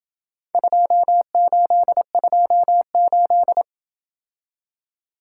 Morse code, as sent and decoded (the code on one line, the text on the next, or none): ..--- ---.. ..--- ---..
2828